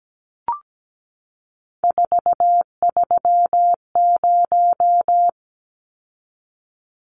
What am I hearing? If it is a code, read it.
Morse code, heard as a keyed tone, about 17 wpm, 700 Hz. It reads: 430
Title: Numbers - 3-Digit 17wpm